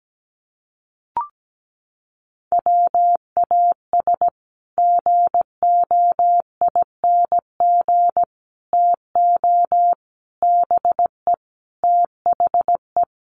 Morse code, heard as a keyed tone, about 17 wpm, 700 Hz. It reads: WAS GOING TO BE THE
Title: Sets of 5 Words Encore2 17wpm